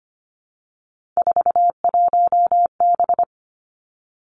Morse code, heard as a keyed tone, about 25 wpm, 700 Hz. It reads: 41B
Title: Three Challenging Letter-Number Combinations 25wpm